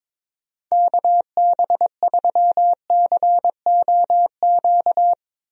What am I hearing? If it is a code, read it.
Morse code, heard as a keyed tone, about 22 wpm, 700 Hz. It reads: KB3COQ
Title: USA Call Signs - 6 Characters - 22wpm